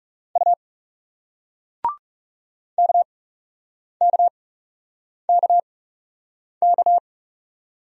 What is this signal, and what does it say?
Morse code, heard as a keyed tone, about 40 wpm, spaced as for 8 wpm, 700 Hz. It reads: U KKKK